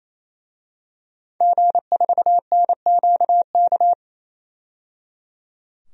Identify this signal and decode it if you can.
Morse code, heard as a keyed tone, about 28 wpm, 700 Hz. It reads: G4NQK
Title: DXCC-100 Call Signs - 5 Characters - 28wpm